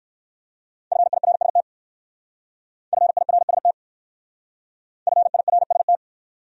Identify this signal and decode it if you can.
Morse code, heard as a keyed tone, about 54 wpm, 700 Hz. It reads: FIRST FIRST FIRST
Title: Speed-Racing Top 200 Words 45wpm